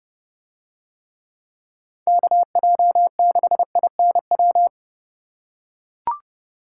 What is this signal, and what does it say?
Morse code, heard as a keyed tone, about 27 wpm, 700 Hz. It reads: KJ6INW